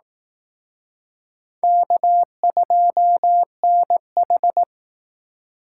Morse code, heard as a keyed tone, about 18 wpm, 700 Hz. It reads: K2NH
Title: Speed Racing - USA 1x2 Call Signs 15wpm